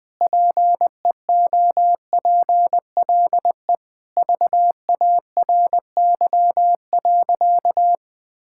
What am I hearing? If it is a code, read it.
Morse code, heard as a keyed tone, about 19 wpm, 700 Hz. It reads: PEOPLE VARY.
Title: Sentences with 2-Word Phrases - 20wpm